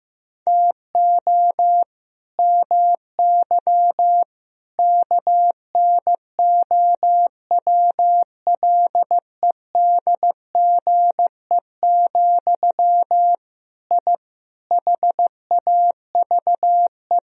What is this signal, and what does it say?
Morse code, heard as a keyed tone, about 16 wpm, 700 Hz. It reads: TO MY KNOWLEDGE, I HAVE
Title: Sentences from Top 1000 Words - Review Part 3 - 15wpm